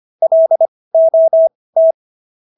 Morse code, heard as a keyed tone, about 25 wpm, spaced as for 18 wpm, 650 Hz. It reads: LOT